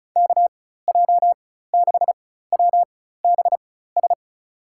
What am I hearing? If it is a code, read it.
Morse code, heard as a keyed tone, about 34 wpm, spaced as for 16 wpm, 700 Hz. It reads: KJ6WBS
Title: USA Call Signs - 6 Characters - 4x Character Spacing - 35wpm